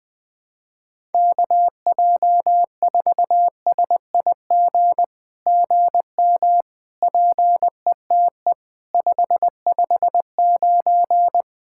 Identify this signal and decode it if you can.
Morse code, heard as a keyed tone, about 18 wpm, 700 Hz. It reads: KJ4SIG GM PETE 559
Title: SOTA QSO Practice 20wpm